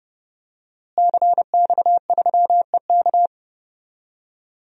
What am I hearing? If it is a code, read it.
Morse code, heard as a keyed tone, about 32 wpm, 700 Hz. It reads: CX3EK
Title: DXCC-100 Call Signs - 5 Characters - Repeated 2x - 30wpm